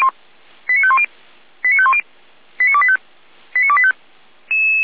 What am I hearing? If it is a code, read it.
Five-tone selective call (ZVEI), heard as quick runs of five tones, tones of about 70 ms. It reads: 89510 89510 89285 89285
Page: /455/SDR/ZVEI/89564/2025.03.07/